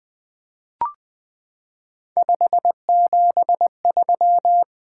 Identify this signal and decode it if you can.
Morse code, heard as a keyed tone, about 20 wpm, 700 Hz. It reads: 573
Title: Three Letter-Number Combinations - 20wpm